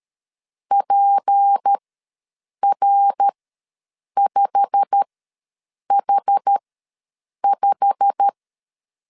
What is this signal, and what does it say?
Morse code, heard as a keyed tone, about 13 wpm, spaced as for 7 wpm, 800 Hz. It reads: PR5H5
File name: LEZIONI / 8 YJ45 / 8 YJ45 ws 12wpm 3146